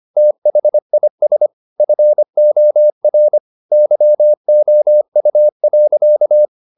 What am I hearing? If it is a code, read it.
Morse code, heard as a keyed tone, about 25 wpm, 600 Hz. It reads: THIS FOR YOU.
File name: the old man and the sea part 1 25wpm